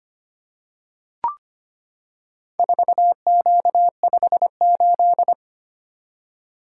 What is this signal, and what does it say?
Morse code, heard as a keyed tone, about 25 wpm, 700 Hz. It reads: 4Q58
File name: practice / 25wpm / Four Letter-Number Combinations - 25wpm